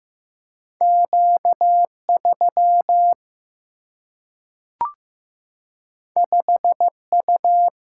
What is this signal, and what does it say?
Morse code, heard as a keyed tone, about 15 wpm, 700 Hz. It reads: Q3 5U
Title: Single-Letter Single-Number 15wpm